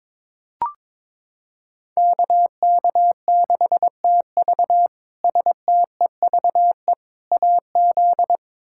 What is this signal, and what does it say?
Morse code, heard as a keyed tone, about 22 wpm, 700 Hz. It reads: KK6TV STEVE AZ
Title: K1USN SST QSO Practice - 22wpm